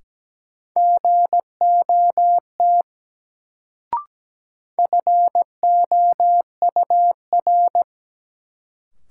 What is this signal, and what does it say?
Morse code, heard as a keyed tone, about 17 wpm, 700 Hz. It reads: GOT FOUR
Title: Sentences from Top 300 Words - New Word Drill - 17wpm